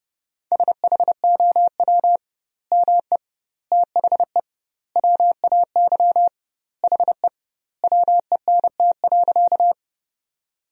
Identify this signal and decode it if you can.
Morse code, heard as a keyed tone, about 32 wpm, 700 Hz. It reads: SHOW ME THE WAY HE WENT.
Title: 2x Word-Spacing - Sentences from Top 200 Words - 30wpm